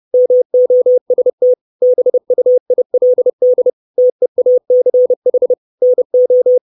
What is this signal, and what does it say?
Morse code, heard as a keyed tone, about 30 wpm, 500 Hz. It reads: MOST BUILD TEACH NO
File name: 632 Words 30WPM